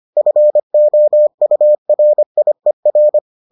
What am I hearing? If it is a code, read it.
Morse code, heard as a keyed tone, about 25 wpm, 600 Hz. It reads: FOURIER